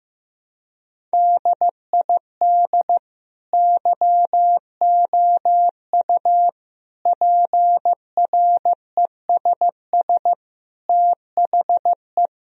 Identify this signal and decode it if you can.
Morse code, heard as a keyed tone, about 15 wpm, 700 Hz. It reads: DID YOU PRESS THE
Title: Sentences from Top 1000 Words - Review Part 1 - 15wpm